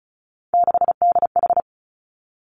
Morse code, heard as a keyed tone, about 36 wpm, 700 Hz. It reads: 6DH
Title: Three Challenging Letter-Number Combinations 35wpm